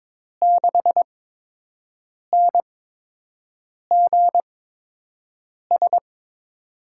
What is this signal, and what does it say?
Morse code, heard as a keyed tone, about 22 wpm, spaced as for 6 wpm, 700 Hz. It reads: 6NGS
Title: USA Call Signs - 5 Characters - 8x Character Spacing - 22wpm